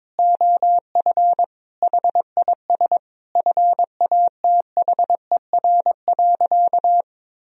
Morse code, heard as a keyed tone, about 22 wpm, 700 Hz. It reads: OF HIS FATHER.
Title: Sentences from Top 1000 Words - Review Part 2 - 22wpm